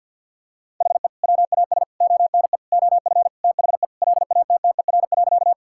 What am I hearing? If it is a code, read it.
Morse code, heard as a keyed tone, about 49 wpm, 700 Hz. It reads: HE WAS ONE OF THE LATTER.
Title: Sentences from Top 1600 Words 50wpm